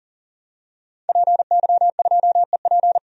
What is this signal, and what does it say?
Morse code, heard as a keyed tone, about 40 wpm, 700 Hz. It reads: PY2EP
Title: DXCC-100 Call Signs - 5 Characters - 40wpm